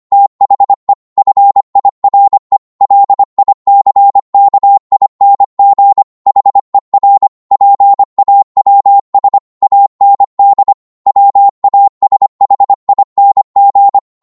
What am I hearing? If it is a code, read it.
Morse code, heard as a keyed tone, about 25 wpm, 800 Hz. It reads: THE FIRE LICKING HER PAWS AND WASHING